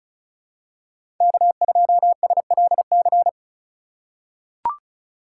Morse code, heard as a keyed tone, about 34 wpm, 700 Hz. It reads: K2SLC
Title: USA Call Signs - 5 Characters - 3x Character Spacing - 35wpm